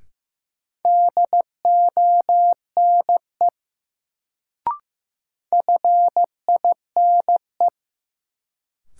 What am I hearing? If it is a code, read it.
Morse code, heard as a keyed tone, about 15 wpm, 700 Hz. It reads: DONE FINE
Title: Sentences from Top 500 Words - New Word Drill - 15wpm